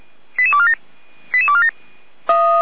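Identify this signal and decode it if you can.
Five-tone selective call (ZVEI), heard as quick runs of five tones, tones of about 70 ms. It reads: 88247 88247
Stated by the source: ZVEI five-tone selective call received on an SDR.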